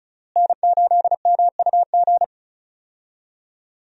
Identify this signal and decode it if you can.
Morse code, heard as a keyed tone, about 34 wpm, 700 Hz. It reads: N8MUG
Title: USA Call Signs - 5 Characters - Repeated 3x - 35wpm